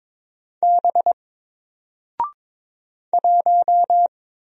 Morse code, heard as a keyed tone, about 22 wpm, 700 Hz. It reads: B 1
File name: Single-Letter-Number-22wpm